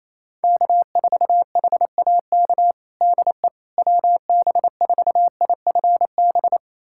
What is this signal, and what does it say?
Morse code, heard as a keyed tone, about 28 wpm, 700 Hz. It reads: K4HAK DE WB4IFB
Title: USA Callsign DE Callsign 28wpm